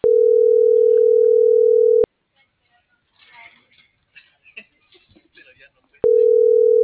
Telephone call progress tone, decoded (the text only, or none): ringback tone